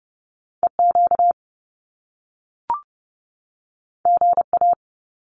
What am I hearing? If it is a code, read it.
Morse code, heard as a keyed tone, about 30 wpm, 700 Hz. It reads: E Q G A
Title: Top Two Letter Combinations in Words 30wpm